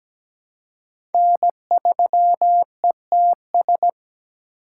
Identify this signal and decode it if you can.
Morse code, heard as a keyed tone, about 17 wpm, 700 Hz. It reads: N3ETS
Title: USA Call Signs - 5 Characters - 4x Character Spacing - 17wpm